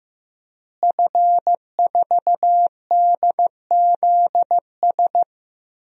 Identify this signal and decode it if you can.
Morse code, heard as a keyed tone, about 15 wpm, 700 Hz. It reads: F4DZS
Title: DXCC-100 Call Signs - 5 Characters - Repeated 3x - 15wpm